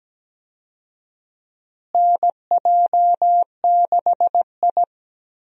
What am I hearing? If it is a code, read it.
Morse code, heard as a keyed tone, about 17 wpm, 700 Hz. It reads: NJ6I
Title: USA Call Signs - 4 Characters - Repeated 2x - 17wpm